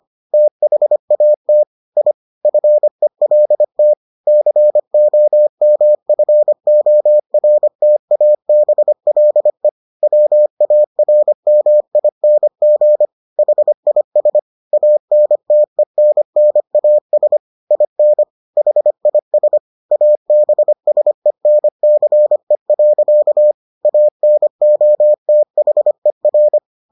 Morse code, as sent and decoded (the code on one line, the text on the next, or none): - .... .- - / .. / ..-. .-.. - / -.-. --- -- ..-. --- .-. - .- -... .-.. / .-- .- .-. -- .. -. --. / .... .. ... / .- -. - . -. -. .- ... / .. -. / .... .. ... / .- -... ... -. -.-. .-.-.- / .- -. --- - .... .-.
THAT I FELT COMFORTABLE WARMING HIS ANTENNAS IN HIS ABSENCE. ANOTHER